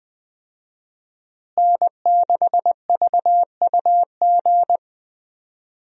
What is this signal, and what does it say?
Morse code, heard as a keyed tone, about 21 wpm, 700 Hz. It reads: N6VUG
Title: USA Call Signs - 5 Characters - Repeated 2x - 20wpm